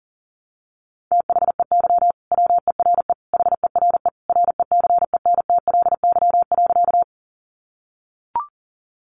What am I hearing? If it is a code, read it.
Morse code, heard as a keyed tone, about 40 wpm, 700 Hz. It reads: THEY WERE HERE RECENTLY.
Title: Sentences from Top 1200 Words 40wpm